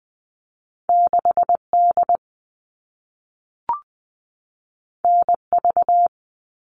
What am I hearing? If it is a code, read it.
Morse code, heard as a keyed tone, about 20 wpm, 700 Hz. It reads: 6D NV
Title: All Cut Numbers - 2-Digit 20wpm